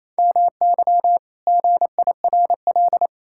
Morse code, heard as a keyed tone, about 28 wpm, 700 Hz. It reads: MY GIRL
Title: Pangram Challenge v2 - 28wpm